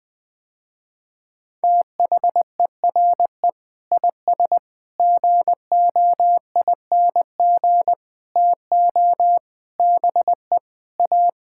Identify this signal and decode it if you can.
Morse code, heard as a keyed tone, about 20 wpm, 700 Hz. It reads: THERE IS GOING TO BE A
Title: Sentences with 7-Word Phrases - Spoken Beginning - 20wpm